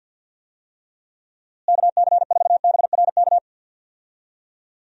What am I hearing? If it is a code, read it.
Morse code, heard as a keyed tone, about 53 wpm, 700 Hz. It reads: KC4BRK
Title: USA Call Signs - 6 Characters - Repeated 2x - 50wpm